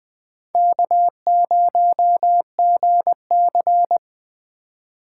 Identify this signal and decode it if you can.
Morse code, heard as a keyed tone, about 22 wpm, 700 Hz. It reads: K0GC
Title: Speed Racing - USA 1x2 Call Signs 20wpm